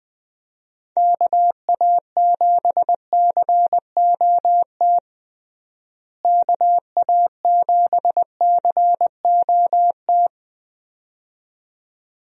Morse code, heard as a keyed tone, about 19 wpm, 700 Hz. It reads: KA7COT KA7COT
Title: USA Call Signs - 6 Characters - Repeated 2x - 20wpm